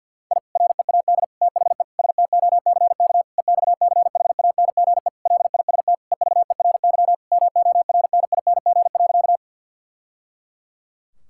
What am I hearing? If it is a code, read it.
Morse code, heard as a keyed tone, about 50 wpm, 700 Hz. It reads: I READ THE STOCK EXCHANGE LIST EVERY MORNING.
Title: Sentences from Top 1000 Words - Review Part 6 - 50wpm